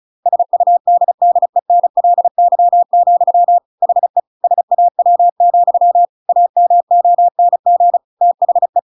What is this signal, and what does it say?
Morse code, heard as a keyed tone, about 34 wpm, 700 Hz. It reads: SUDDENLY, HE SAW, AMONG THE